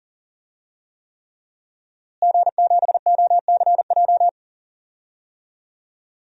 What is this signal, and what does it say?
Morse code, heard as a keyed tone, about 40 wpm, 700 Hz. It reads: G7OCJ
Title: DXCC-100 Call Signs - 5 Characters - Repeated 3x - 40wpm